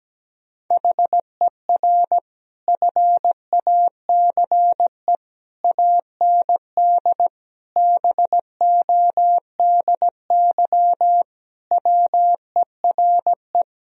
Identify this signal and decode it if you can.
Morse code, heard as a keyed tone, about 17 wpm, 700 Hz. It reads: HER FACE AND BODY WERE
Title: Sentences from Top 1000 Words - Review Part 2 - 17wpm